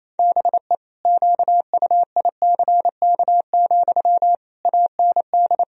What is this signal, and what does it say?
Morse code, heard as a keyed tone, about 27 wpm, 700 Hz. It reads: BE QUICK, AND
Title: Sentences from Top 1000 Words - Review Part 3 - 28wpm